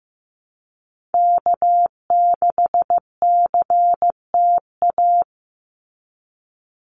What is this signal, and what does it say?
Morse code, heard as a keyed tone, about 15 wpm, 700 Hz. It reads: K6CTA
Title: Call Signs Repeated 2x - 15wpm